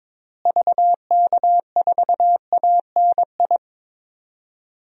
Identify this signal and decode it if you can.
Morse code, heard as a keyed tone, about 22 wpm, 700 Hz. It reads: VK4ANI
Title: DXCC-100 Call Signs - 6 Characters - Repeated 2x - 22wpm